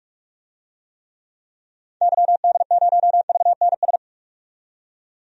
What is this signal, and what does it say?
Morse code, heard as a keyed tone, about 41 wpm, 700 Hz. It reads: YD0VNS
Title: DXCC-100 Call Signs - 6 Characters - 45wpm